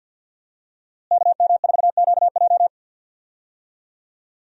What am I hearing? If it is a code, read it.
Morse code, heard as a keyed tone, about 48 wpm, 700 Hz. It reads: KM4QJ
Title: USA Call Signs - 5 Characters - Repeated 2x - 50wpm